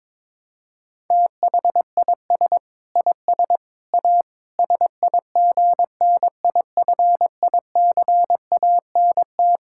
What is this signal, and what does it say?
Morse code, heard as a keyed tone, about 21 wpm, 700 Hz. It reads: THIS IS A SIGNIFICANT
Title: Sentences from Top 1000 Words - Review Part 6 - 22wpm